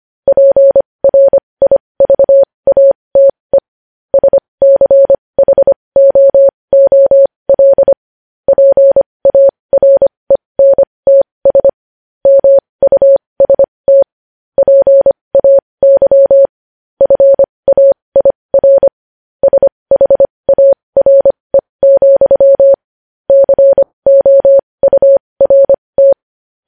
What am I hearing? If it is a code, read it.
Morse code, heard as a keyed tone, about 25 wpm, spaced as for 20 wpm, 550 Hz. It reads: PRIVATE SCHOOL PARENTS MUST PAY FAIR SHARE, COURT
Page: news headlines in morse code at 20 wpm